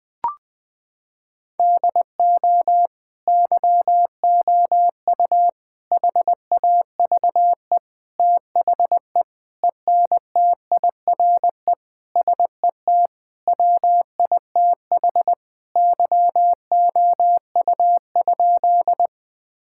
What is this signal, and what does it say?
Morse code, heard as a keyed tone, about 20 wpm, 700 Hz. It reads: DO YOU HAVE THE ENTIRE SET WITH YOU?
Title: Sentences from Top 1000 Words - Review Part 5 - 20wpm